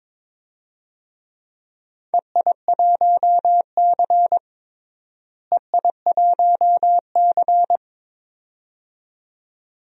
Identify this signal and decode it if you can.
Morse code, heard as a keyed tone, about 22 wpm, 700 Hz. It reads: EI1C EI1C
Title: DXCC-100 Call Signs - 4 Characters - Repeated 2x - 22wpm